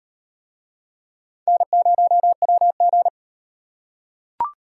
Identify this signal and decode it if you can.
Morse code, heard as a keyed tone, about 37 wpm, 700 Hz. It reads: N0WG